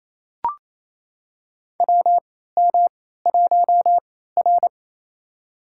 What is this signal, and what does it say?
Morse code, heard as a keyed tone, about 28 wpm, spaced as for 16 wpm, 700 Hz. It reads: WM1R
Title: USA Call Signs - 4 Characters - 3x Character Spacing - 28wpm